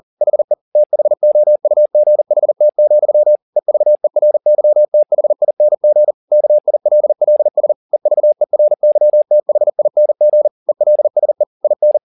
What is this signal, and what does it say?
Morse code, heard as a keyed tone, about 40 wpm, 600 Hz. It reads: HE THOUGHT, EVERYTHING KILLS EVERYTHING ELSE IN
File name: the old man and the sea 2 40wpm